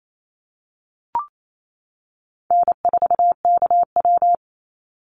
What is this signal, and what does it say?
Morse code, heard as a keyed tone, about 28 wpm, 700 Hz. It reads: N4KW
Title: Easy Call Signs - 28wpm